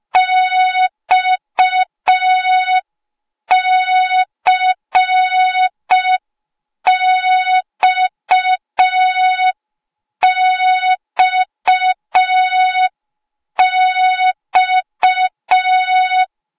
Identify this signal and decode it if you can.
Morse code, heard as a keyed tone, about 5 wpm, 3700 Hz. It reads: XCXXX
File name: Morze-HA2MN / hang / 08KXC HA2MN